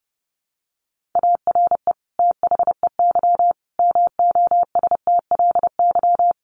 Morse code, heard as a keyed tone, about 30 wpm, 700 Hz. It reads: ARE THEY MOSTLY
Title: Sentences from Top 1300 Words 30wpm